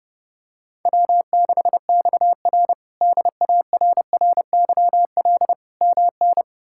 Morse code, heard as a keyed tone, about 30 wpm, 700 Hz. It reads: W6XR DARRYL MN
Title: K1USN SST QSO Practice - 30wpm